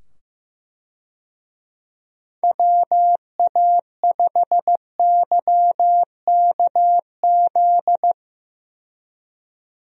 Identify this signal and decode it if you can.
Morse code, heard as a keyed tone, about 16 wpm, 700 Hz. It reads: WA5YKZ
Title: USA Call Signs - 6 Characters - 15wpm